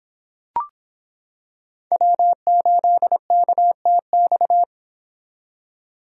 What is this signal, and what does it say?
Morse code, heard as a keyed tone, about 25 wpm, 700 Hz. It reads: W8KTX